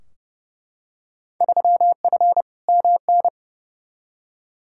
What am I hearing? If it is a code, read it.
Morse code, heard as a keyed tone, about 30 wpm, 700 Hz. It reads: 3F MN
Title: Field Day Exchanges Encore - 30wpm